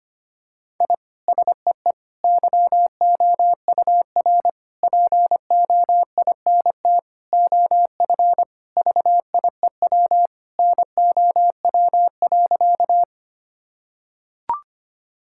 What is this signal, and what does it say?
Morse code, heard as a keyed tone, about 25 wpm, 700 Hz. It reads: I SEE YOUR POINT OF VIEW NOW.
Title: Sentences from Top 1000 Words - Review Part 6 - 25wpm